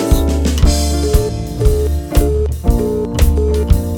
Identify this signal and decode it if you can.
Busy tone (425 Hz, short), with noise.